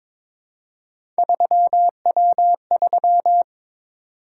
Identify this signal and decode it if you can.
Morse code, heard as a keyed tone, about 22 wpm, 700 Hz. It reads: 3W3
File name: practice / 22wpm / Three Letter-Number Combinations - 22wpm